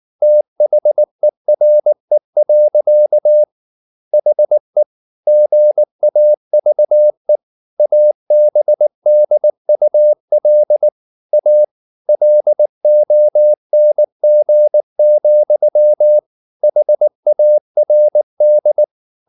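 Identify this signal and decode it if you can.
Morse code, heard as a keyed tone, about 19 wpm, 600 Hz. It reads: THERE. HE GAVE ABDUL A LONG, HARD
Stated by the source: Morse code at 19 wpm from CW med Gnister, reading Ken Follett's novel Never in English.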